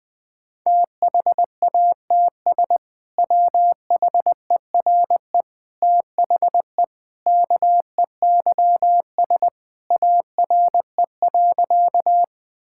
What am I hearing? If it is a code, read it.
Morse code, heard as a keyed tone, about 21 wpm, 700 Hz. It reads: THATS WHERE THE KEYS ARE.
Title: Sentences with 2-Word Phrases - 20wpm